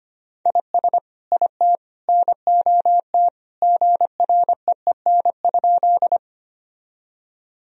Morse code, heard as a keyed tone, about 25 wpm, 700 Hz. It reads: IS IT NOT GREEN?
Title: Sentences from Top 1000 Words - Review Part 2 - 25wpm